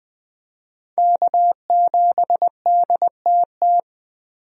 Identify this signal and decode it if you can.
Morse code, heard as a keyed tone, about 21 wpm, 700 Hz. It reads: K7DTT
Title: USA Call Signs - 5 Characters - Repeated 3x - 20wpm